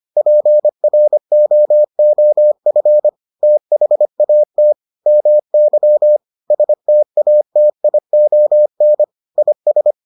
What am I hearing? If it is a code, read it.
Morse code, heard as a keyed tone, about 25 wpm, 600 Hz. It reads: PROOF THAT MY STATION IS